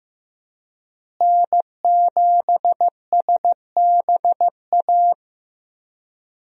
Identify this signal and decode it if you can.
Morse code, heard as a keyed tone, about 16 wpm, 700 Hz. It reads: N7SBA
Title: USA Call Signs - 5 Characters - Repeated 3x - 15wpm